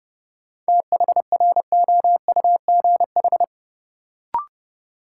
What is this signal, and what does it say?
Morse code, heard as a keyed tone, about 30 wpm, 700 Hz. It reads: THROUGH
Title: Sets of 2 Words - New Word Drill - 30wpm